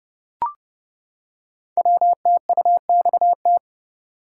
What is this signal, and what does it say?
Morse code, heard as a keyed tone, about 30 wpm, 700 Hz. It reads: WTUXT
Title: Code Groups - 5 Letters - 30wpm